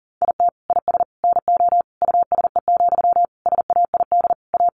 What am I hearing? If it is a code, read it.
Morse code, heard as a keyed tone, about 40 wpm, 700 Hz. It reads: IT IS NO USE, SAID A
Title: Sentences from Top 500 Words 40wpm